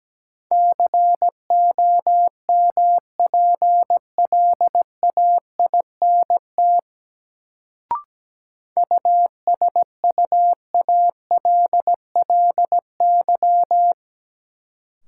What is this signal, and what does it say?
Morse code, heard as a keyed tone, about 17 wpm, 700 Hz. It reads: COMPLAINT USUALLY
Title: Sentences from Top 1400 Words - New Word Drill Alt - 17wpm